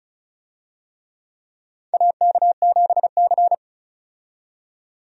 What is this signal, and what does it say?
Morse code, heard as a keyed tone, about 36 wpm, 700 Hz. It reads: AK7C
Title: USA Call Signs - 4 Characters - 35wpm